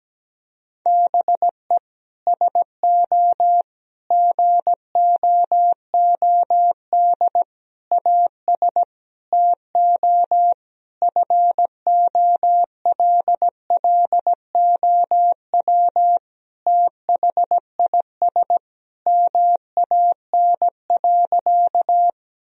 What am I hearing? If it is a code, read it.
Morse code, heard as a keyed tone, about 17 wpm, 700 Hz. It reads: BE SO GOOD AS TO FOLLOW THIS MAN.
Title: Sentences from Top 1000 Words - Review Part 1 - 17wpm